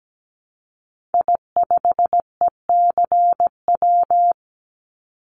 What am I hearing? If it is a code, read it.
Morse code, heard as a keyed tone, about 17 wpm, 700 Hz. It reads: I5ECW
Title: Call Signs - 17wpm